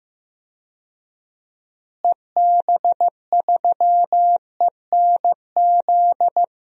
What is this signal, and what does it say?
Morse code, heard as a keyed tone, about 15 wpm, 700 Hz. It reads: EB3ENZ
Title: DXCC-100 Call Signs - 6 Characters - Repeated 2x - 15wpm